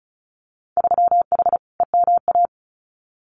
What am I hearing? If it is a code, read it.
Morse code, heard as a keyed tone, about 36 wpm, 700 Hz. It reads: 3H EMA
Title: Winter Field Day Exchanges Encore 35wpm